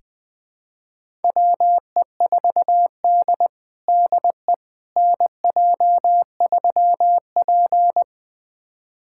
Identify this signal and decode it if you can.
Morse code, heard as a keyed tone, about 20 wpm, 700 Hz. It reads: WE4D DE NJ3P
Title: USA Callsign DE Callsign - 4 Characters - 20wpm